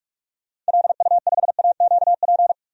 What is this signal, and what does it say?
Morse code, heard as a keyed tone, about 47 wpm, 700 Hz. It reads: LU5AQP
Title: DXCC-100 Call Signs - 6 Characters - 45wpm